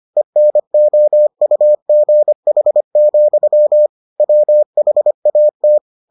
Morse code, heard as a keyed tone, about 25 wpm, 600 Hz. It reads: ENOUGH, WHAT